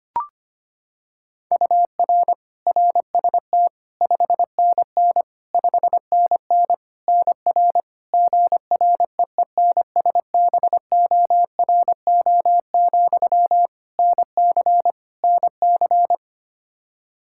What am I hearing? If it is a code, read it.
Morse code, heard as a keyed tone, about 25 wpm, 700 Hz. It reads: UR RST 5NN 5NN NR GREENSBORO, NC NC
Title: Practice QSO - Exchange 1 - Beginning - 25wpm